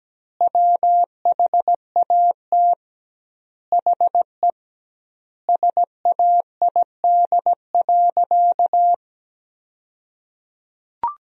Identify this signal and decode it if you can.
Morse code, heard as a keyed tone, about 17 wpm, 700 Hz. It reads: WHAT HE SAID.
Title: 2x Word-Spacing - Sentences from Top 400 Words - 17wpm